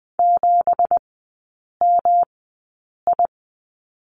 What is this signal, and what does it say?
Morse code, heard as a keyed tone, about 20 wpm, spaced as for 8 wpm, 700 Hz. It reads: 7MI
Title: 178 Lesson 29e - Slash - Three Characters 20wpm